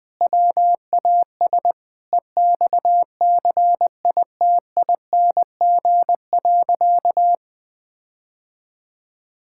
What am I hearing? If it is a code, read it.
Morse code, heard as a keyed tone, about 20 wpm, 700 Hz. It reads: WAS EXCITING.
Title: Sentences with 5-Word Phrases - 20wpm